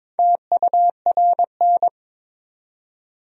Morse code, heard as a keyed tone, about 22 wpm, 700 Hz. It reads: TURN
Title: 4x Word-Spacing - Sentences from Top 200 Words - 22wpm